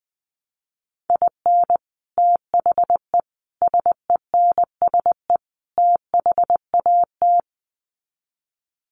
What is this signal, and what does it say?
Morse code, heard as a keyed tone, about 20 wpm, 700 Hz. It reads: IN THE SENSE THAT
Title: Sets of 4 Words 20wpm